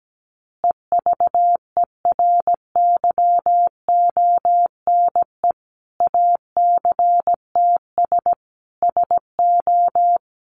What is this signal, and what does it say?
Morse code, heard as a keyed tone, about 17 wpm, 700 Hz. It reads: EVERYONE ACTS SO
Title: Easy Phrases 17wpm